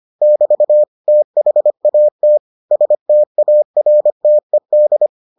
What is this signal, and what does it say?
Morse code, heard as a keyed tone, about 25 wpm, 600 Hz. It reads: <BT> THAT STARTED